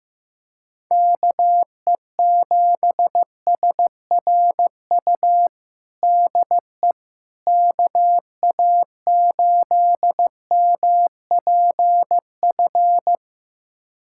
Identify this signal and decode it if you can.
Morse code, heard as a keyed tone, about 15 wpm, 700 Hz. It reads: KE7SRU DE KA8MPF
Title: USA Callsign DE Callsign 15wpm